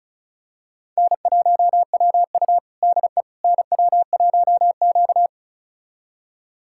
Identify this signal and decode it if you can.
Morse code, heard as a keyed tone, about 34 wpm, 700 Hz. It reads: N1WU DE NW1Q
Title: USA Callsign DE Callsign - 4 Characters - 35wpm